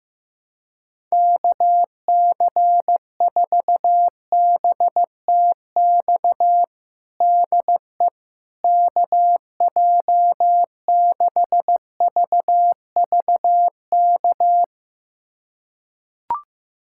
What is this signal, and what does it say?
Morse code, heard as a keyed tone, about 15 wpm, 700 Hz. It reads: KC4BTX DE KJ6VVK
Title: USA Callsign DE Callsign - 6 Characters - 15wpm